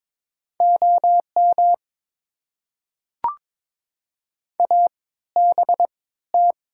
Morse code, heard as a keyed tone, about 22 wpm, spaced as for 15 wpm, 690 Hz. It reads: OM ABT
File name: Top 90 QSO Elements - 3x Character Spacing - 22wpm